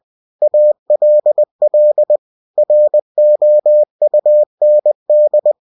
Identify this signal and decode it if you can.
Morse code, heard as a keyed tone, about 20 wpm, 600 Hz. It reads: ALL ROUND
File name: cw-books / winnie-the-pooh / winnie-20wpm-page-0094